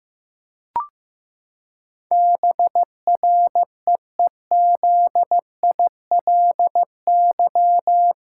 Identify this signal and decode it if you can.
Morse code, heard as a keyed tone, about 15 wpm, 700 Hz. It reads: BREEZILY
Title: Pangram Challenge v2 - 15wpm